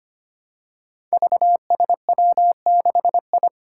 Morse code, heard as a keyed tone, about 25 wpm, 700 Hz. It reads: VSW6I
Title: Five Challenging Letter-Number Combinations 25wpm